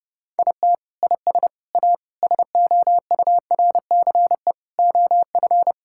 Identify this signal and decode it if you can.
Morse code, heard as a keyed tone, about 32 wpm, 700 Hz. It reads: IT IS A SOURCE OF
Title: Sentences from Top 1000 Words - Review Part 5 - 30wpm